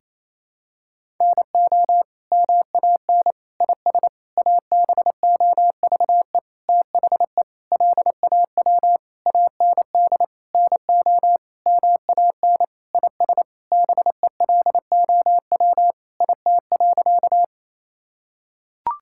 Morse code, heard as a keyed tone, about 28 wpm, 700 Hz. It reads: NO MAN IS ABOVE THE LAW AND NO MAN IS BELOW IT.
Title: Sentences from Top 1000 Words - Review Part 5 - 28wpm